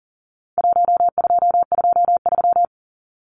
Morse code, heard as a keyed tone, about 40 wpm, 700 Hz. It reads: 1223
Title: Numbers - Contest Style - 4-Digit 40wpm